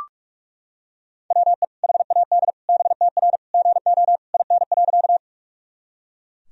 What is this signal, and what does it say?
Morse code, heard as a keyed tone, about 41 wpm, 700 Hz. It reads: WE HAD BTR GO IN.
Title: Sentences from Top 400 Words and Abbreviations 45wpm